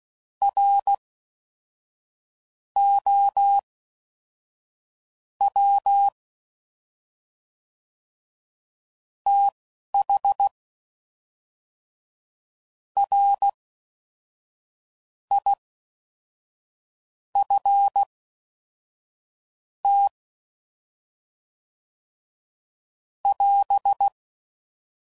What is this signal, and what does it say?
Morse code, heard as a keyed tone, about 16 wpm, spaced as for 5 wpm, 800 Hz. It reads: ROW THRIFT <AS>